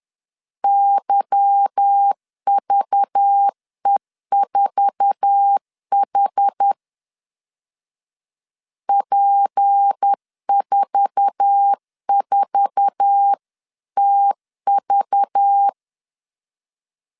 Morse code, heard as a keyed tone, about 10 wpm, 800 Hz. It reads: YVE4H P44TV
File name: LEZIONI / 8 YJ45 / 8 YJ4 10wpm 2633